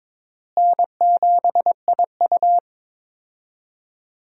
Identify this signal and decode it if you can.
Morse code, heard as a keyed tone, about 22 wpm, 700 Hz. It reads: N7IU
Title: USA Call Signs - 4 Characters - Repeated 3x - 22wpm